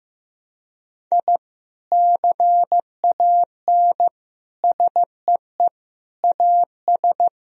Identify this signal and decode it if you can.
Morse code, heard as a keyed tone, about 15 wpm, 700 Hz. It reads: I CAN SEE AS
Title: Sentences from Top 1000 Words - Review Part 2 - 15wpm